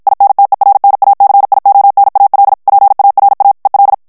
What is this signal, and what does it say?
Morse code, heard as a keyed tone, about 56 wpm, 800 Hz. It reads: INTERNACIONAL PARA EL
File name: ESPAGNOL TEXTE EN CLAIR 1 VITESSE 50 MOTS MINUTES